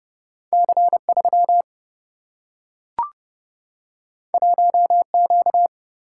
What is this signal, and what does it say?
Morse code, heard as a keyed tone, about 30 wpm, 700 Hz. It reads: C3 1Q
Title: Single-Letter Single-Number 30wpm